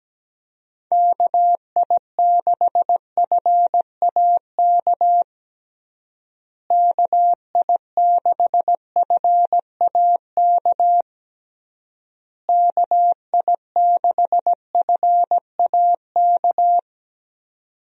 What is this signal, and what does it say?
Morse code, heard as a keyed tone, about 17 wpm, 700 Hz. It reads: KI6FAK KI6FAK KI6FAK
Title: USA Call Signs - 6 Characters - Repeated 3x - 17wpm